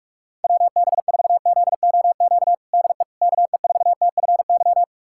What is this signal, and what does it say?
Morse code, heard as a keyed tone, about 42 wpm, 700 Hz. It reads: WB4ZOQ DE KE4TFY
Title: USA Callsign DE Callsign 45wpm